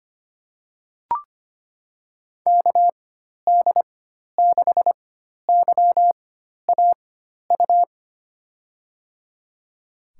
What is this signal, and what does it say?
Morse code, heard as a keyed tone, about 25 wpm, spaced as for 12 wpm, 700 Hz. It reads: KD6YAU
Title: USA Call Signs - 6 Characters - 4x Character Spacing - 25wpm